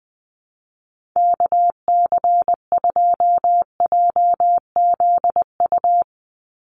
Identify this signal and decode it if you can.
Morse code, heard as a keyed tone, about 20 wpm, 700 Hz. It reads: KC2JZU
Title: USA Call Signs 20wpm